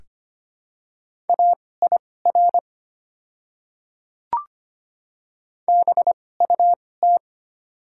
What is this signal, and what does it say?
Morse code, heard as a keyed tone, about 25 wpm, spaced as for 18 wpm, 700 Hz. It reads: AIR BUT
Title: 2x Character Spacing - Top 200 Words - 25wpm